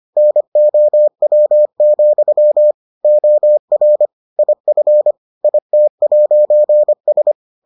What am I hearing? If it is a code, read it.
Morse code, heard as a keyed tone, about 25 wpm, 600 Hz. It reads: NOW, OR IF IT'S